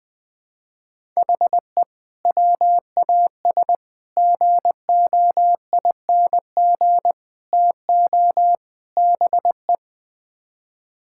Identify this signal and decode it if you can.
Morse code, heard as a keyed tone, about 20 wpm, 700 Hz. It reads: HE WAS GOING TO BE
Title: Build A Phrase - 5-Word Phrases - 20wpm